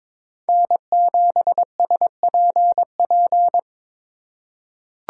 Morse code, heard as a keyed tone, about 22 wpm, 700 Hz. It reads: N7SPP
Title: USA Call Signs - 2x Character Spacing - 22wpm